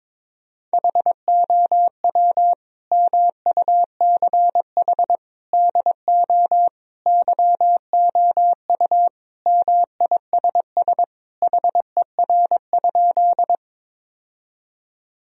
Morse code, heard as a keyed tone, about 22 wpm, 700 Hz. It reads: HOW MUCH DO YOU MISS HER?
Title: Sentences from Top 1000 Words - Review Part 3 - 22wpm